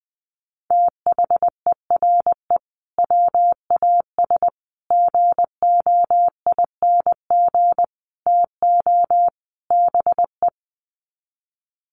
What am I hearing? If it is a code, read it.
Morse code, heard as a keyed tone, about 20 wpm, 700 Hz. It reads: THERE WAS GOING TO BE
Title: Sets of 5 Words Encore2 20wpm